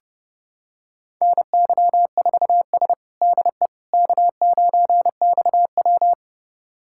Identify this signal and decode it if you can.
Morse code, heard as a keyed tone, about 32 wpm, 700 Hz. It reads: NY4S DE K9XW
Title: USA Callsign DE Callsign - 4 Characters - 30wpm